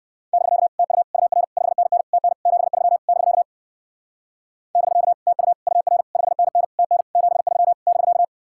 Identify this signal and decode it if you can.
Morse code, heard as a keyed tone, about 56 wpm, 700 Hz. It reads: <BK> TU UR 5NN TN 73 <BK> <BK> TU UR 5NN TN 73 <BK>
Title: Speed-Racing Common QSO Phrases - 45wpm